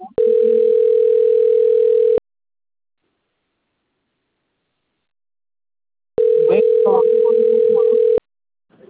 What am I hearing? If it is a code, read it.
Ringback tone (North America), 440 + 480 Hz.